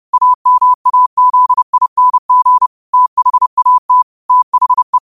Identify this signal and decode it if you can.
Morse code, heard as a keyed tone, about 30 wpm, 1000 Hz. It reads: AMAZING THAT THE
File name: 30wpm